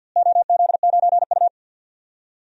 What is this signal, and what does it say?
Morse code, heard as a keyed tone, about 50 wpm, 700 Hz. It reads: OZ9U
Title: DXCC-100 Call Signs - 4 Characters - Repeated 2x - 50wpm